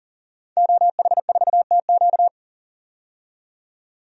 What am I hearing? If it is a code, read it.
Morse code, heard as a keyed tone, about 40 wpm, 700 Hz. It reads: OH4TQ
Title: DXCC-100 Call Signs - 5 Characters - 40wpm